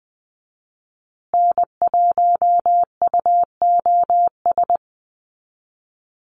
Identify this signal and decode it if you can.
Morse code, heard as a keyed tone, about 20 wpm, 700 Hz. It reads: N1UOS
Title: USA Call Signs Repeated 3x 20wpm